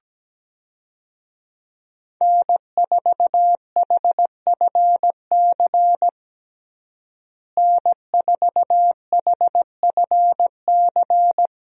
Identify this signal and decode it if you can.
Morse code, heard as a keyed tone, about 17 wpm, 700 Hz. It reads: N4HFC N4HFC
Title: USA Call Signs - 5 Characters - Repeated 3x - 17wpm